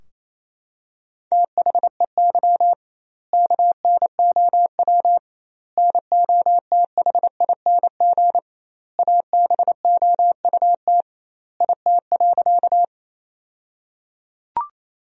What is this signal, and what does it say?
Morse code, heard as a keyed tone, about 28 wpm, 700 Hz. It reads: THEY KNOW NOTHING ABOUT IT.
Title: 2x Word-Spacing - Sentences from Top 500 Words - 28wpm